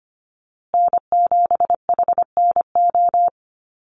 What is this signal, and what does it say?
Morse code, heard as a keyed tone, about 25 wpm, 700 Hz. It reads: N7HNO
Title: USA Call Signs Repeated 2x 25wpm